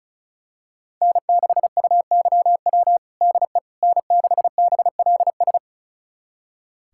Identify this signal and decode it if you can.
Morse code, heard as a keyed tone, about 34 wpm, 700 Hz. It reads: N6UYW DE N6BLS
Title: USA Callsign DE Callsign - 5 Characters - 35wpm